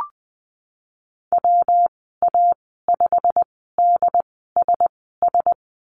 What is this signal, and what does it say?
Morse code, heard as a keyed tone, about 20 wpm, spaced as for 15 wpm, 700 Hz. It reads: WA5DSS
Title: 122 Lesson 20g - Number 3 - Call Signs 20wpm